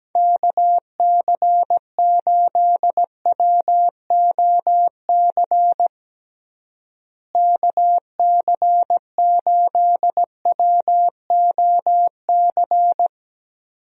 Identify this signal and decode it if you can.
Morse code, heard as a keyed tone, about 17 wpm, 700 Hz. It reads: KC8WOC KC8WOC
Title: USA Call Signs - 6 Characters - Repeated 2x - 17wpm